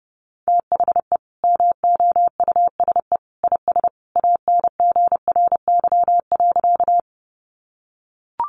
Morse code, heard as a keyed tone, about 30 wpm, 700 Hz. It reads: THE MOUSE IS ANGRY.
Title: Easy Phrases 30wpm